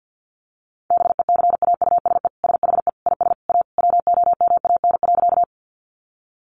Morse code, heard as a keyed tone, about 50 wpm, 700 Hz. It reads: BECAUSE SHE IS A WOMAN.
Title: Sentences from Top 1100 Words 50wpm